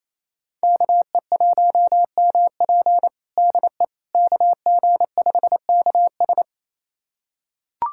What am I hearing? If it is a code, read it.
Morse code, heard as a keyed tone, about 27 wpm, 690 Hz. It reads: KE1MP DE KG5KS
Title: USA Callsign DE Callsign - 5 Characters - 28wpm